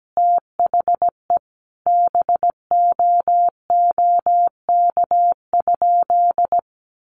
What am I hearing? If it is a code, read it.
Morse code, heard as a keyed tone, about 17 wpm, 700 Hz. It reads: THE BOOK?
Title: Sentences from Top 1300 Words 17wpm